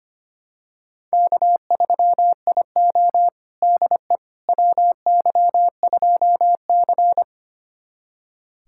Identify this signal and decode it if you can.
Morse code, heard as a keyed tone, about 25 wpm, 700 Hz. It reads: K3IO DE WY2C
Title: USA Callsign DE Callsign - 4 Characters - 25wpm